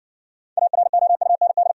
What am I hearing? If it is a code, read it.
Morse code, heard as a keyed tone, about 60 wpm, 690 Hz. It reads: AROUND